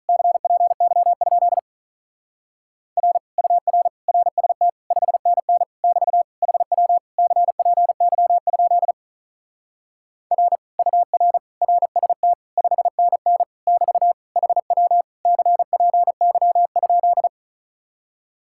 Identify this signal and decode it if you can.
Morse code, heard as a keyed tone, about 46 wpm, 700 Hz. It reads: CPY? R UR RST 5NN <BT> HW CPY? R UR RST 5NN <BT> HW CPY?